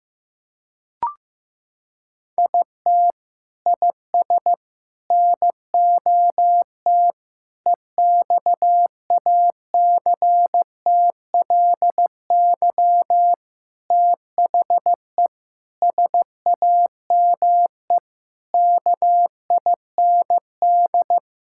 Morse code, heard as a keyed tone, about 15 wpm, 700 Hz. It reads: IT IS NOT EXACTLY THE SAME KIND
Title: Sentences from Top 1000 Words - Review Part 6 - 15wpm